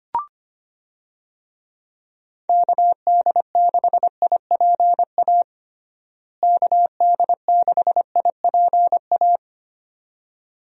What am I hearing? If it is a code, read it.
Morse code, heard as a keyed tone, about 25 wpm, 700 Hz. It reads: KD6IPA KD6IPA
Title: USA Call Signs - 6 Characters - Repeated 2x - 25wpm